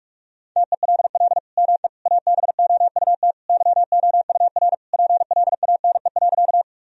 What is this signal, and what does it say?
Morse code, heard as a keyed tone, about 41 wpm, 700 Hz. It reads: TELL ME ABOUT YOUR PLANE.
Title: Sentences from Top 1000 Words - Review Part 3 - 45wpm